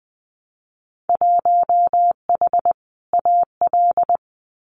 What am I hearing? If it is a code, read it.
Morse code, heard as a keyed tone, about 20 wpm, 700 Hz. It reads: 1H AL
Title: Winter Field Day Exchanges 20wpm